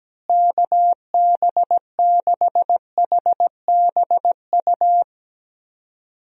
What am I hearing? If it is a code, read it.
Morse code, heard as a keyed tone, about 17 wpm, 700 Hz. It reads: KB6HBU
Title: USA Call Signs - 6 Characters - Repeated 2x - 17wpm